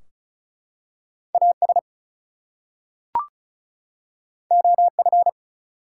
Morse code, heard as a keyed tone, about 36 wpm, 700 Hz. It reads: AS OF
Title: Two Letter Words - 35wpm